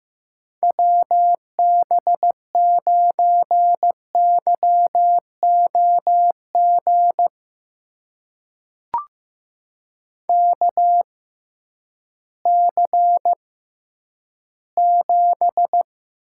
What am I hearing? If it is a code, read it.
Morse code, heard as a keyed tone, about 15 wpm, spaced as for 10 wpm, 700 Hz. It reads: WB9YOG KC7
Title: USA Call Signs - 6 Characters - 6x Character Spacing - 15wpm